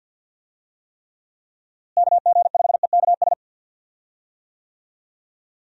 Morse code, heard as a keyed tone, about 52 wpm, 700 Hz. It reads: KG5EKS
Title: USA Call Signs - 6 Characters - Repeated 2x - 50wpm